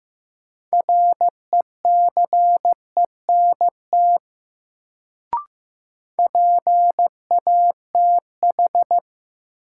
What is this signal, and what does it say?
Morse code, heard as a keyed tone, about 15 wpm, 700 Hz. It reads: RECENT PATH
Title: Sentences from Top 1400 Words - New Word Drill Alt - 15wpm